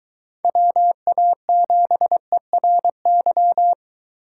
Morse code, heard as a keyed tone, about 22 wpm, 700 Hz. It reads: WA7ERY